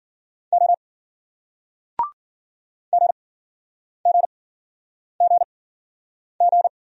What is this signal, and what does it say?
Morse code, heard as a keyed tone, about 53 wpm, spaced as for 9 wpm, 700 Hz. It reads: O GGGG